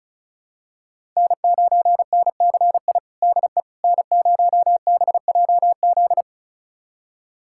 Morse code, heard as a keyed tone, about 36 wpm, 700 Hz. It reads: N9NCI DE N0BJZ
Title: USA Callsign DE Callsign - 5 Characters - 35wpm